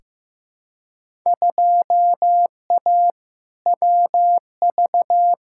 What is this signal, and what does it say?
Morse code, heard as a keyed tone, about 15 wpm, 700 Hz. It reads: 2A WV
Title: Field Day Exchanges Encore - 15wpm